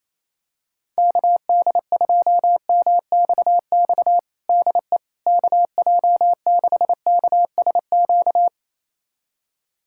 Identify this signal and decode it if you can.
Morse code, heard as a keyed tone, about 28 wpm, 700 Hz. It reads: KD2MXX DE KJ6KSQ
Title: USA Callsign DE Callsign 28wpm